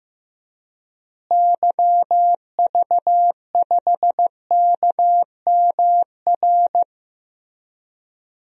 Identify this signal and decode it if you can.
Morse code, heard as a keyed tone, about 15 wpm, 700 Hz. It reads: YV5KMR
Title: DXCC-100 Call Signs - 6 Characters - Repeated 3x - 15wpm